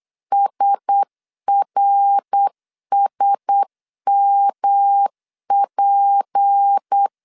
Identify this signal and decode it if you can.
Morse code, heard as a keyed tone, about 8 wpm, 800 Hz. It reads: SRSMP
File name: CW / DET / 1 9 1500C S A 8wpm 5637